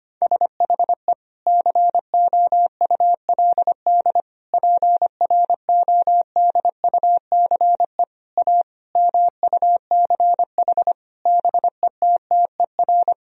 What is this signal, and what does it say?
Morse code, heard as a keyed tone, about 25 wpm, 700 Hz. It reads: SHE COULD PRODUCE A MUCH BETTER
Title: Sentences from Top 1000 Words - Review Part 3 - 25wpm